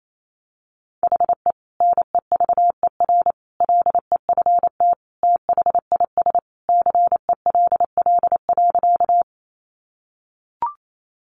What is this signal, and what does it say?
Morse code, heard as a keyed tone, about 28 wpm, 700 Hz. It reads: HE NEVER LEFT THIS CELL.
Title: Sentences from Top 1100 Words 28wpm